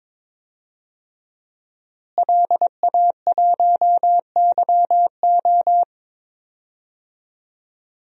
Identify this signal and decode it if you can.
Morse code, heard as a keyed tone, about 22 wpm, 700 Hz. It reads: LA1YO
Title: DXCC-100 Call Signs - 5 Characters - Repeated 3x - 22wpm